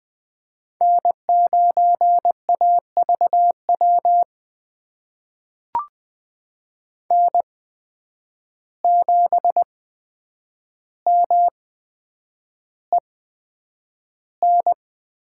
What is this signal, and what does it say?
Morse code, heard as a keyed tone, about 20 wpm, spaced as for 9 wpm, 700 Hz. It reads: N9AVW N7MEN
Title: USA Call Signs - 5 Characters - 8x Character Spacing - 20wpm